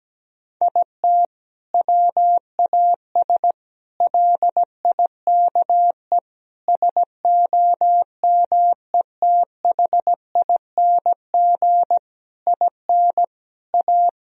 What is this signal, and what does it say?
Morse code, heard as a keyed tone, about 17 wpm, 700 Hz. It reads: IT WAS LIKE SOMETHING IN A
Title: Sentences from Top 1000 Words - Review Part 1 - 17wpm